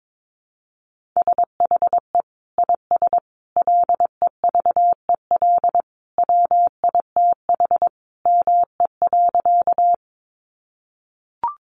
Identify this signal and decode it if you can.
Morse code, heard as a keyed tone, about 22 wpm, 700 Hz. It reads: SHE IS LEVEL WITH ME.
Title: Sentences from Top 700 Words 22wpm